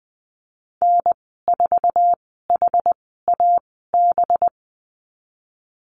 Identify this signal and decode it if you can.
Morse code, heard as a keyed tone, about 20 wpm, spaced as for 15 wpm, 700 Hz. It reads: N4HAB
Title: 186 Lesson 30g - Letter b - Call Signs 20wpm